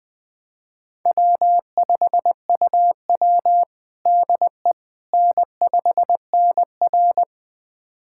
Morse code, heard as a keyed tone, about 20 wpm, 700 Hz. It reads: W5UW DE N5NR
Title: USA Callsign DE Callsign - 4 Characters - 20wpm